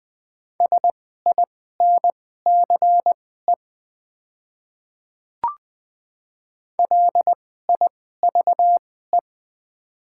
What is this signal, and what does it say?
Morse code, heard as a keyed tone, about 20 wpm, spaced as for 14 wpm, 700 Hz. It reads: SINCE LIVE
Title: 2x Character Spacing - Top 300 Words - 20wpm